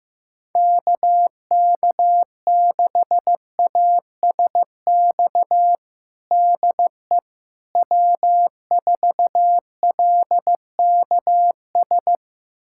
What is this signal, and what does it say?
Morse code, heard as a keyed tone, about 15 wpm, 700 Hz. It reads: KK6ASX DE W4LKS
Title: USA Callsign DE Callsign 15wpm